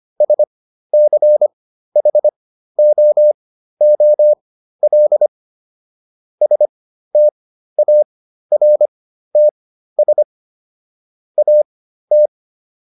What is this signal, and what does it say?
Morse code, heard as a keyed tone, about 25 wpm, spaced as for 13 wpm, 600 Hz. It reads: SCHOOL STARTS AT